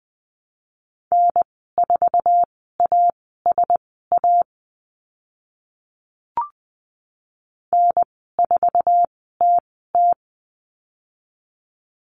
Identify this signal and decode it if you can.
Morse code, heard as a keyed tone, about 20 wpm, spaced as for 14 wpm, 700 Hz. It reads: N4ASA N4TT
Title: 082 Lesson 14g - Number 2 - Call Signs 20wpm